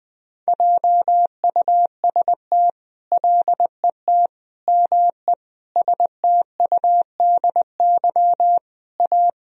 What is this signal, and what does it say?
Morse code, heard as a keyed tone, about 19 wpm, 700 Hz. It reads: JUST LET ME STUDY A
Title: Sentences from Top 1000 Words - Review Part 1 - 20wpm